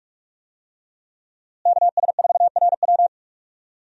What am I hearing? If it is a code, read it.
Morse code, heard as a keyed tone, about 43 wpm, 700 Hz. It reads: KS4RW
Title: USA Call Signs - 5 Characters - Repeated 3x - 45wpm